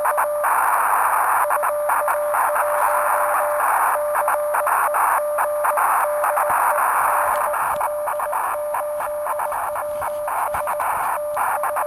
Two-tone teleprinter signal, about 420 Hz shift.
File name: attachments / 24.07.2018 DL8WS mag LOOP 60sm